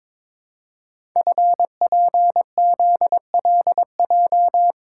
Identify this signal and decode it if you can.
Morse code, heard as a keyed tone, about 23 wpm, 700 Hz. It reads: FPZLJ
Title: Code Groups - 5 Letter:Numbers - 22wpm